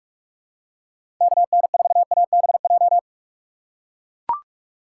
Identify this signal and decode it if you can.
Morse code, heard as a keyed tone, about 44 wpm, 700 Hz. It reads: KN4ABJ
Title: USA Call Signs - 6 Characters - 6x Character Spacing - 45wpm